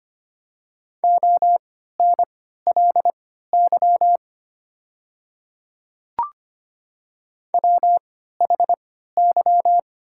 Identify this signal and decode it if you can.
Morse code, heard as a keyed tone, about 25 wpm, spaced as for 14 wpm, 690 Hz. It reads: ONLY WHY
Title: 3x Character Spacing - Top 200 Words - 25wpm